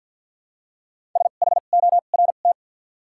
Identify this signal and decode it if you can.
Morse code, heard as a keyed tone, about 53 wpm, spaced as for 37 wpm, 700 Hz. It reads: SHORT
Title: 2x Character Spacing - Top 500 Words - 50wpm